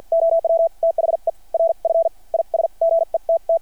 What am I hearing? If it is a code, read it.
Morse code, heard as a keyed tone, about 53 wpm, spaced as for 44 wpm, 660 Hz. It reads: OW THE RF IS GETT
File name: archived / CRSnet